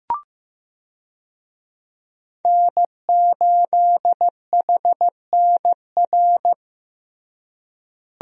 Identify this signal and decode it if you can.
Morse code, heard as a keyed tone, about 15 wpm, 700 Hz. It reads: N8HNR
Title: USA Call Signs - 5 Characters - Repeated 3x - 15wpm